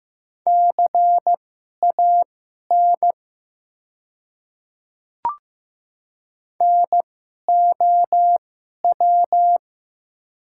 Morse code, heard as a keyed tone, about 15 wpm, spaced as for 11 wpm, 700 Hz. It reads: CAN NOW
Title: 2x Character Spacing - Top 100 Words - 15wpm